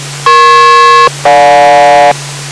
Aircraft SELCAL, heard as two pairs of tones. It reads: EP-GJ